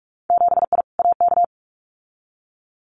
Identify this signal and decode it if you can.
Morse code, heard as a keyed tone, about 46 wpm, 700 Hz. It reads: 7I AK
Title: Winter Field Day Exchanges Encore 45wpm